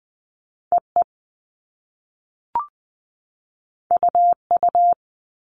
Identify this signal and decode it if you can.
Morse code, heard as a keyed tone, about 20 wpm, 700 Hz. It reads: E E U U
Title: All Cut Numbers - 2-Digit 20wpm